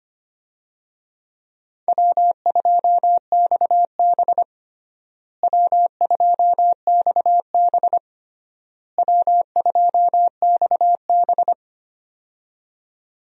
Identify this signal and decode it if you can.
Morse code, heard as a keyed tone, about 25 wpm, 700 Hz. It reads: W2XB W2XB W2XB
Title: USA Call Signs - 4 Characters - Repeated 3x - 25wpm